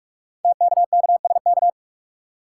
Morse code, heard as a keyed tone, about 44 wpm, 690 Hz. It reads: TKKSK
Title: Code Groups - 5 Letters - 45wpm